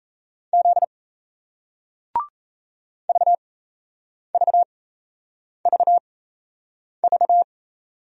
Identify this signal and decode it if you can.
Morse code, heard as a keyed tone, about 37 wpm, spaced as for 8 wpm, 700 Hz. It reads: Z VVVV